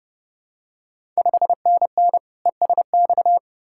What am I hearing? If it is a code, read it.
Morse code, heard as a keyed tone, about 32 wpm, 700 Hz. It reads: 5NN ESX
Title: Ontario QSO Party Exchange 30wpm